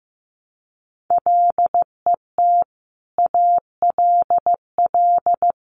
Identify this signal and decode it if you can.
Morse code, heard as a keyed tone, about 15 wpm, 700 Hz. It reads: LET ALL
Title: Sentences from Top 1100 Words 15wpm